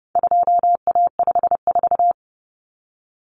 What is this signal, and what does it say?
Morse code, heard as a keyed tone, about 30 wpm, 700 Hz. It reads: 2A54
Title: Common Cut Numbers - 4-Digit 30wpm